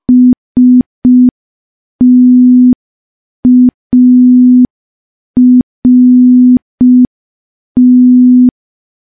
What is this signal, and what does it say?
Morse code, heard as a keyed tone, about 5 wpm, 250 Hz. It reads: START
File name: morse / morse-training-05WPM